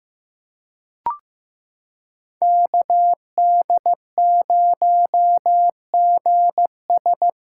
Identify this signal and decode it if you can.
Morse code, heard as a keyed tone, about 15 wpm, 700 Hz. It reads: KD0GS
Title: USA Callsign DE Callsign - 5 Characters - 15wpm